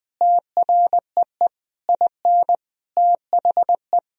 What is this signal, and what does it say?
Morse code, heard as a keyed tone, about 20 wpm, 700 Hz. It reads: TREE IN THE
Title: Sentences from Top 1000 Words - Review Part 1 - 20wpm